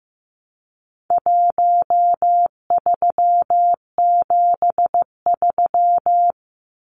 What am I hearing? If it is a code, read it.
Morse code, heard as a keyed tone, about 15 wpm, 700 Hz. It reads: 1373
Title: Numbers - Contest Style - 4-Digit 15wpm